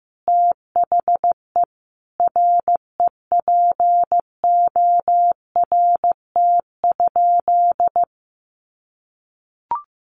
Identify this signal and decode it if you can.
Morse code, heard as a keyed tone, about 15 wpm, 700 Hz. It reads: THE REPORT?
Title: Sentences from Top 900 Words 15wpm